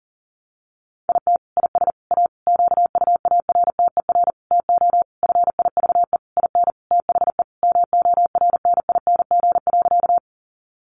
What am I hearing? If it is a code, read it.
Morse code, heard as a keyed tone, about 40 wpm, 700 Hz. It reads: IT IS A QUARTER TO FIVE IN THE MORNING.
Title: Sentences from Top 1300 Words 40wpm